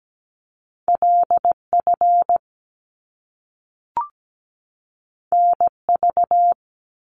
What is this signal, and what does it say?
Morse code, heard as a keyed tone, about 17 wpm, 700 Hz. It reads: LF NV
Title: Top Two Letter Combinations in Words 17wpm